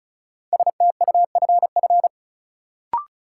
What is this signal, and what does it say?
Morse code, heard as a keyed tone, about 36 wpm, 700 Hz. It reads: STUFF
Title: Sentences from Top 1400 Words - New Word Drill - 35wpm